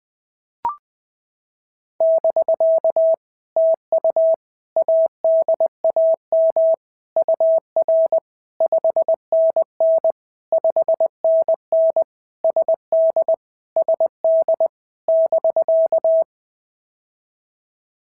Morse code, heard as a keyed tone, about 19 wpm, 650 Hz. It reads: <BK> TU ADAM UR 5NN 5NN SD SD <BK>
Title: POTA QSO Practice - 20wpm